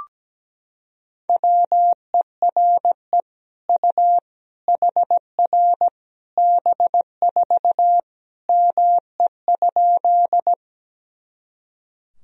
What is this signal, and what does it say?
Morse code, heard as a keyed tone, about 17 wpm, 700 Hz. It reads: WERE U HR B4 ME?
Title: Sentences from Top 200 Words and Abbreviations 17wpm